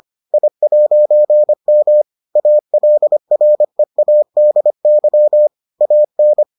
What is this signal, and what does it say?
Morse code, heard as a keyed tone, about 25 wpm, 600 Hz. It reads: I'M ALREADY AN